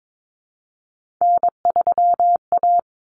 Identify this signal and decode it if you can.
Morse code, heard as a keyed tone, about 22 wpm, 700 Hz. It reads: N3A
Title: Common Cut Numbers - 3-Digit 22wpm